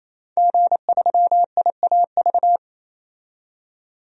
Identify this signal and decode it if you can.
Morse code, heard as a keyed tone, about 28 wpm, 700 Hz. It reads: G3IAV
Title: DXCC-100 Call Signs - 5 Characters - 28wpm